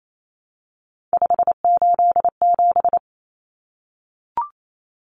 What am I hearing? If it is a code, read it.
Morse code, heard as a keyed tone, about 28 wpm, 700 Hz. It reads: 587
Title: Numbers - 3-Digit 28wpm